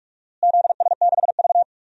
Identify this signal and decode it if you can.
Morse code, heard as a keyed tone, about 46 wpm, 700 Hz. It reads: ZS6V
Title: DXCC-100 Call Signs - 4 Characters - Repeated 2x - 45wpm